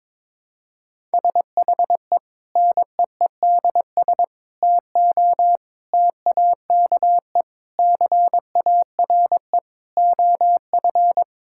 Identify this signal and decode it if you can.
Morse code, heard as a keyed tone, about 22 wpm, 700 Hz. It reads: SHE NEEDS TO TAKE CARE OF
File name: practice / 22wpm / Sentences with 4-Word Phrases - 22wpm